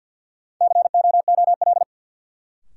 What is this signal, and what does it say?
Morse code, heard as a keyed tone, about 53 wpm, 700 Hz. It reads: COOL
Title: Sentences from Top 1200 Words - New Word Drill Alt - 50wpm